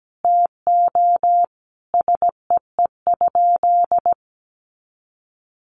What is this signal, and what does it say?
Morse code, heard as a keyed tone, about 17 wpm, 700 Hz. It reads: TO SEE?
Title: Sentences from Top 100 Words 17wpm